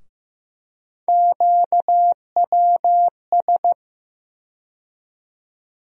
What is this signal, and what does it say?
Morse code, heard as a keyed tone, about 16 wpm, 700 Hz. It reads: QWS
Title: USA Call Sign Postfixes - 3 Letters 15wpm